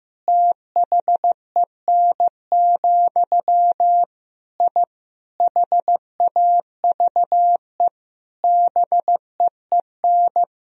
Morse code, heard as a keyed tone, about 15 wpm, 700 Hz. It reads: THEN, I HAVE BEEN
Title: Sentences from Top 1000 Words - Review Part 3 - 15wpm